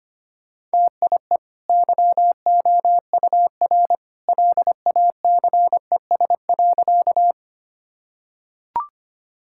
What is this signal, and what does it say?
Morse code, heard as a keyed tone, about 25 wpm, 700 Hz. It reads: TIE YOUR LACES.
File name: Easy Sentences with Hints - 25wpm